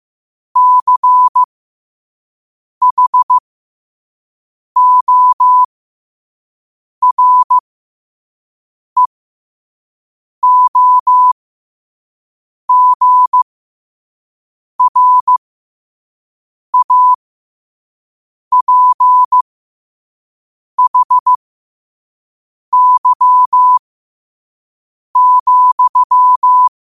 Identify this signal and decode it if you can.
Morse code, heard as a keyed tone, about 15 wpm, spaced as for 5 wpm, 1000 Hz. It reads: CHOREOGRAPHY,
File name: qotdfiles / 5wpm